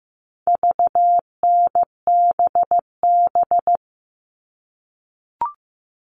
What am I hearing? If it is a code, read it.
Morse code, heard as a keyed tone, about 15 wpm, 700 Hz. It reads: VNBB
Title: All Cut Numbers - 4-Digit 15wpm